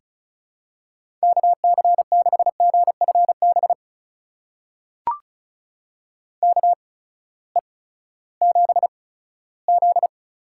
Morse code, heard as a keyed tone, about 34 wpm, spaced as for 17 wpm, 700 Hz. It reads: KC6GFB KE7Z
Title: USA Call Signs - 6 Characters - 8x Character Spacing - 35wpm